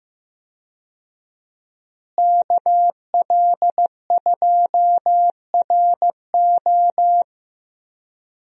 Morse code, heard as a keyed tone, about 15 wpm, 700 Hz. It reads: KL2RO
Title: USA Call Signs - 5 Characters - Repeated 3x - 15wpm